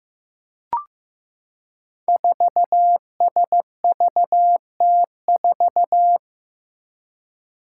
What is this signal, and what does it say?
Morse code, heard as a keyed tone, about 15 wpm, 700 Hz. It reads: 4SVT4
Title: Five Challenging Letter-Number Combinations 15wpm